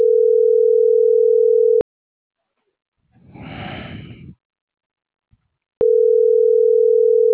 Ringback tone (North America), 440 + 480 Hz.